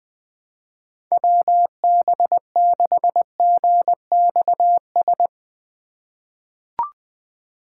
Morse code, heard as a keyed tone, about 21 wpm, 700 Hz. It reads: WB6GXS
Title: USA Call Signs - 6 Characters - 4x Character Spacing - 20wpm